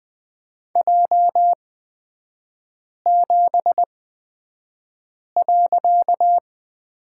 Morse code, heard as a keyed tone, about 20 wpm, spaced as for 5 wpm, 700 Hz. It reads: J7.
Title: 255 Lesson 41e - Period - Three Characters 20wpm